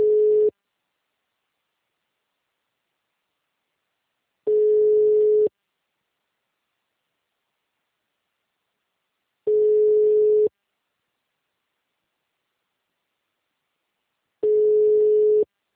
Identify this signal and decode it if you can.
Ringback tone (Europe, 425 Hz).